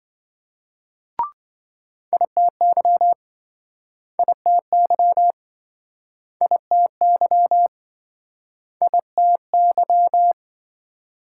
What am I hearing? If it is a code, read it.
Morse code, heard as a keyed tone, about 26 wpm, 700 Hz. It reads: ITY ITY ITY ITY